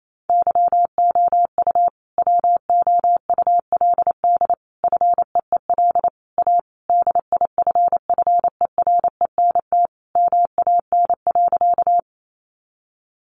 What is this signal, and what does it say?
Morse code, heard as a keyed tone, about 28 wpm, 700 Hz. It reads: YOU WOULD FEEL A DIFFERENT MAN.
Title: Sentences from Top 600 Words 28wpm